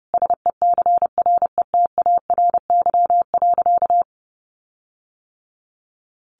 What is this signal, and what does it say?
Morse code, heard as a keyed tone, about 30 wpm, 700 Hz. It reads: SECRETARY.
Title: Sentences from Top 1300 Words 30wpm